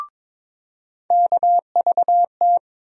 Morse code, heard as a keyed tone, about 21 wpm, 700 Hz. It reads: KVT
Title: USA Call Sign Postfixes - 3 Letters 22wpm